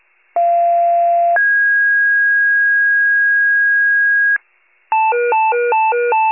Two-tone page, heard as a one-second tone, then a three-second tone, about 690 then 1650 Hz. It is heard from 0.4 to 4.4 s.
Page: 41 RESCUE DUTY